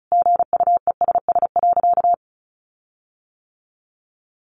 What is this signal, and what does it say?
Morse code, heard as a keyed tone, about 34 wpm, 700 Hz. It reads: GUESS.
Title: Sentences from Top 1500 Words 35wpm